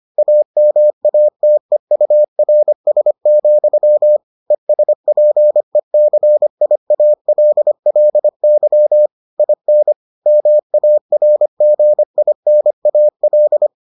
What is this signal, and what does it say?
Morse code, heard as a keyed tone, about 25 wpm, 600 Hz. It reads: AMATEURS, ESPECIALLY IN MARGINAL